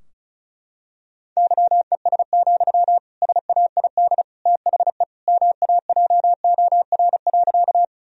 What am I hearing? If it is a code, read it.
Morse code, heard as a keyed tone, about 36 wpm, 700 Hz. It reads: YES, SAID THE MAJOR.
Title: Sentences from Top 1000 Words - Review Part 4 - 35wpm